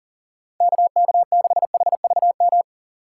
Morse code, heard as a keyed tone, about 37 wpm, 700 Hz. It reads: KK6HVM
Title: USA Call Signs - 6x Character Spacing - 40wpm